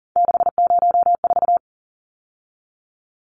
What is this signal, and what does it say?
Morse code, heard as a keyed tone, about 40 wpm, 700 Hz. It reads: 604
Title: Numbers - 3-Digit 40wpm